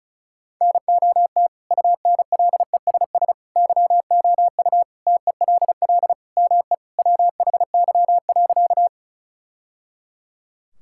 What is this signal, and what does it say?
Morse code, heard as a keyed tone, about 34 wpm, 700 Hz. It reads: NOT UNLESS YOU TELL ME WHY.
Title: Sentences from Top 1000 Words - Review Part 6 - 35wpm